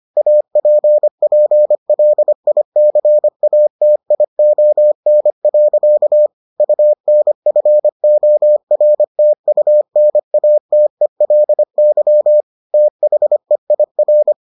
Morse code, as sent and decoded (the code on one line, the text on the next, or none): .- .--. .--. .-.. .. -.-. .- - .. --- -. .-.-.- / ..- -. ..-. --- .-. - ..- -. .- - . .-.. -.-- / - .... .. .-.
APPLICATION. UNFORTUNATELY THEIR